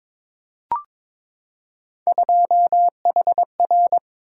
Morse code, heard as a keyed tone, about 22 wpm, 700 Hz. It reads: 2HR
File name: practice / 22wpm / Three Letter-Number Combinations - 22wpm